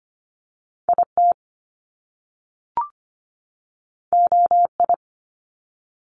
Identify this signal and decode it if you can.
Morse code, heard as a keyed tone, about 25 wpm, 700 Hz. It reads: IT OI
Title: Top Two Letter Combinations in Words 25wpm